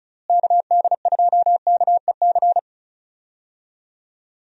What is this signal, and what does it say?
Morse code, heard as a keyed tone, about 36 wpm, 700 Hz. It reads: KD2KEC
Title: USA Call Signs - 6 Characters - Repeated 2x - 35wpm